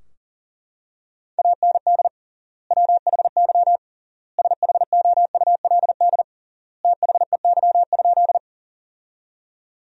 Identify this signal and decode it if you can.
Morse code, heard as a keyed tone, about 40 wpm, 700 Hz. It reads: AND WHY SHOULD THEY?
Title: 3x Word-Spacing - Sentences from Top 300 Words - 40wpm